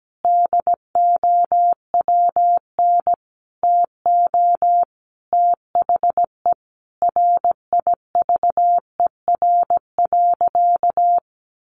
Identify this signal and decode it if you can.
Morse code, heard as a keyed tone, about 17 wpm, 700 Hz. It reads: DOWN TO THE RIVER.
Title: Sentences from Top 1400 Words 17wpm